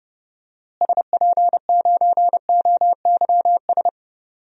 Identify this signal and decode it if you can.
Morse code, heard as a keyed tone, about 30 wpm, 700 Hz. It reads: SP9OYS
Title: DXCC-100 Call Signs - 6 Characters - Repeated 2x - 30wpm